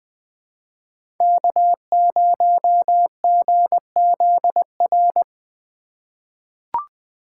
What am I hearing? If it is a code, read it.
Morse code, heard as a keyed tone, about 21 wpm, 700 Hz. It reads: K0GZR
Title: USA Call Signs - 5 Characters - 2x Character Spacing - 20wpm